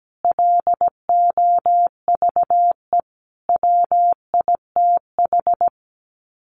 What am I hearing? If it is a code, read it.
Morse code, heard as a keyed tone, about 17 wpm, 700 Hz. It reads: LOVE WITH
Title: Sets of 5 Words Encore 17wpm